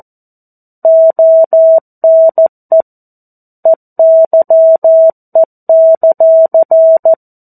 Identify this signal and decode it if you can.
Morse code, heard as a keyed tone, about 14 wpm, 650 Hz. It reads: ONE EYE;
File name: 02-Chapter-02 14 WPM